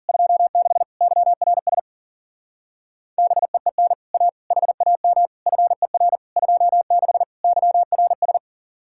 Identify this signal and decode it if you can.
Morse code, heard as a keyed tone, about 47 wpm, 700 Hz. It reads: 26 YRS BEEN A HAM FER 26 YRS
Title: Speed-Racing Common QSO Phrases Encore - 40wpm